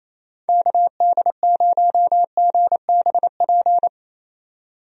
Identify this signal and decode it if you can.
Morse code, heard as a keyed tone, about 28 wpm, 700 Hz. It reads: KD0GBP
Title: USA Call Signs - 6 Characters - 28wpm